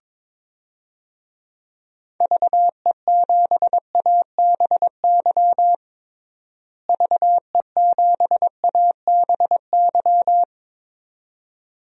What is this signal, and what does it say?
Morse code, heard as a keyed tone, about 22 wpm, 700 Hz. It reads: VE7ABY VE7ABY
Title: DXCC-100 Call Signs - 6 Characters - Repeated 2x - 22wpm